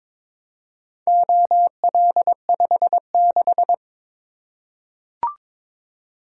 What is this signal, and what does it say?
Morse code, heard as a keyed tone, about 22 wpm, 700 Hz. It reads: OL56
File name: practice / 22wpm / Four Letter-Number Combinations - 22wpm